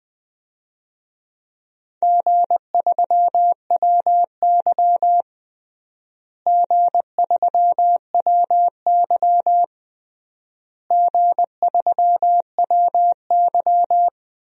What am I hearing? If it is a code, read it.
Morse code, heard as a keyed tone, about 21 wpm, 700 Hz. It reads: G3WY G3WY G3WY
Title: DXCC-100 Call Signs - 4 Characters - Repeated 3x - 20wpm